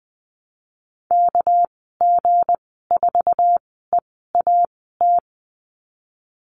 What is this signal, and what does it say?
Morse code, heard as a keyed tone, about 20 wpm, spaced as for 14 wpm, 700 Hz. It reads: KG4EAT
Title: 200 Lesson 32g - Letter k - Call Signs 20wpm